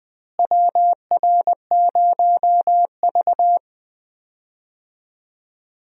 Morse code, heard as a keyed tone, about 20 wpm, 700 Hz. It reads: WR0V
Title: USA Call Signs - 4 Characters - Repeated 2x - 20wpm